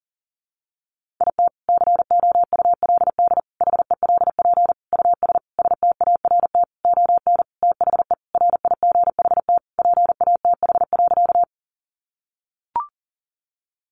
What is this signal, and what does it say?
Morse code, heard as a keyed tone, about 40 wpm, 700 Hz. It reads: IT COULD HELP US START ON THE RIGHT PATH.
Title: Sentences from Top 1400 Words 40wpm